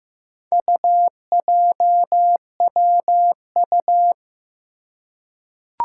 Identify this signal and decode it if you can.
Morse code, heard as a keyed tone, about 15 wpm, 700 Hz. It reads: UJWU
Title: Four Challenging Letter-Number Combinations 15wpm